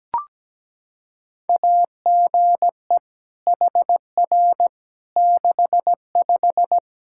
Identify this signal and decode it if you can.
Morse code, heard as a keyed tone, about 17 wpm, 700 Hz. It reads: AGE HR 65
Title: Practice QSO - Exchange 3 - More Info - 17wpm